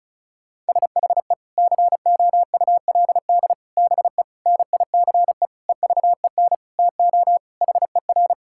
Morse code, heard as a keyed tone, about 36 wpm, 700 Hz. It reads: SHE COULD BE NICE EVEN TO HER
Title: Sentences from Top 1600 Words 35wpm